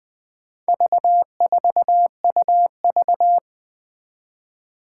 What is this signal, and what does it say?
Morse code, heard as a keyed tone, about 20 wpm, 700 Hz. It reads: V4UV
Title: Four Challenging Letter-Number Combinations 20wpm